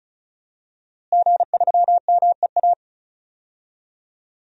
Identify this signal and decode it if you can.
Morse code, heard as a keyed tone, about 36 wpm, 700 Hz. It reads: G3MEA
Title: DXCC-100 Call Signs - 5 Characters - Repeated 2x - 35wpm